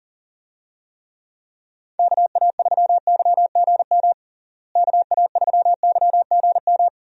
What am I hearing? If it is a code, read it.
Morse code, heard as a keyed tone, about 40 wpm, 700 Hz. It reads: KA3YGM KA3YGM
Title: USA Call Signs - 6 Characters - Repeated 3x - 40wpm